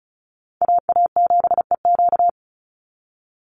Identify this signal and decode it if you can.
Morse code, heard as a keyed tone, about 34 wpm, 700 Hz. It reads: AA7EQ
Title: NA SOTA Chaser Callsigns 35wpm